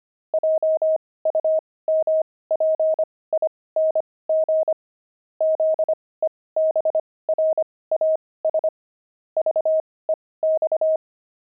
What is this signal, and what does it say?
Morse code, heard as a keyed tone, about 25 wpm, spaced as for 18 wpm, 630 Hz. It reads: JUMPING ZEBRAS VEX
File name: pangram-fs-18-cs-25-daft